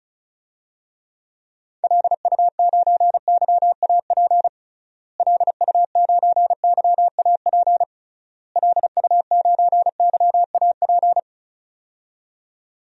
Morse code, heard as a keyed tone, about 34 wpm, 700 Hz. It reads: LU9YAP LU9YAP LU9YAP
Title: DXCC-100 Call Signs - 6 Characters - Repeated 3x - 35wpm